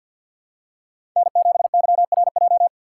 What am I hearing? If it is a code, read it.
Morse code, heard as a keyed tone, about 47 wpm, 700 Hz. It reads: N7YRJ
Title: USA Call Signs - 3x Character Spacing - 50wpm